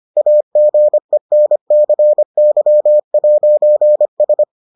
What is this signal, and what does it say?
Morse code, heard as a keyed tone, about 25 wpm, 600 Hz. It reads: AGENCY'S